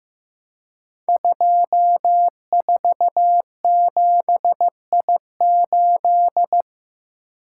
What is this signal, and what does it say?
Morse code, heard as a keyed tone, about 15 wpm, 700 Hz. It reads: 247I8
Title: Code Groups - 5 Letter:Numbers - 15wpm